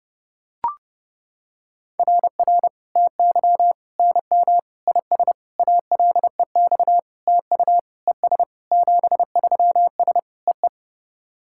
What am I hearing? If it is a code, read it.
Morse code, heard as a keyed tone, about 30 wpm, 700 Hz. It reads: RR TY NM IS ALEX TU ES 73S EE
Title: SOTA QSO Practice 30wpm